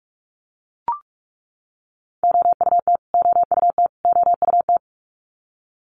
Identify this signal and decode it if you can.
Morse code, heard as a keyed tone, about 44 wpm, 700 Hz. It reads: OUT OUT OUT
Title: Top 500 Words Repeated 45wpm